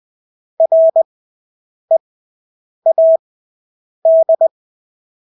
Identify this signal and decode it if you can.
Morse code, heard as a keyed tone, about 20 wpm, spaced as for 8 wpm, 650 Hz. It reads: READ
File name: Engelske ord 20/8 wpm